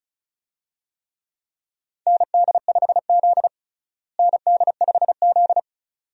Morse code, heard as a keyed tone, about 36 wpm, 700 Hz. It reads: ND5Z ND5Z
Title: USA Call Signs - 4 Characters - Repeated 2x - 35wpm